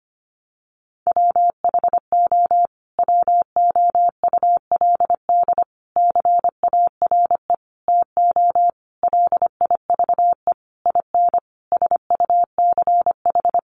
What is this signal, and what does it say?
Morse code, heard as a keyed tone, about 25 wpm, 700 Hz. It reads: WHO WOULD CARE TO LIVE IN SUCH
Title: Sentences from Top 1500 Words 25wpm